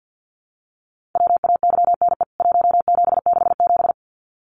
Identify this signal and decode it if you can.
Morse code, heard as a keyed tone, about 50 wpm, 700 Hz. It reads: WAYNE 1767
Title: CWT Exchange 50wpm